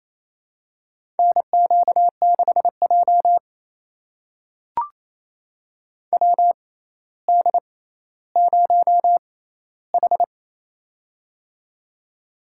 Morse code, heard as a keyed tone, about 28 wpm, spaced as for 14 wpm, 700 Hz. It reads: NQ6J WD0H
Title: USA Call Signs - 4 Characters - 6x Character Spacing - 28wpm